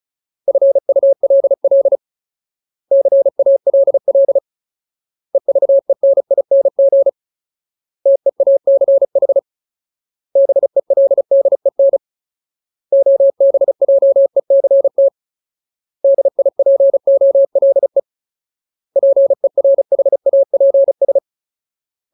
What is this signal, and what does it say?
Morse code, heard as a keyed tone, about 34 wpm, 550 Hz. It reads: FULL CALL EVENING TEACH BELDEN OBJECT DIPOLE PERHAPS
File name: Ultimate Word List / 35WPM Ultimate